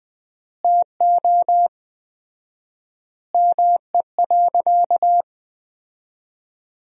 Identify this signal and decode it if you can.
Morse code, heard as a keyed tone, about 21 wpm, 700 Hz. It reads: TO ME.
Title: 4x Word-Spacing - Sentences from Top 500 Words - 20wpm